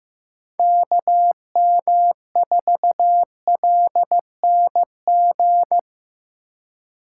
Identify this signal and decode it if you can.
Morse code, heard as a keyed tone, about 15 wpm, 700 Hz. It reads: KM4LNG
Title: USA Call Signs - 6x Character Spacing - 15wpm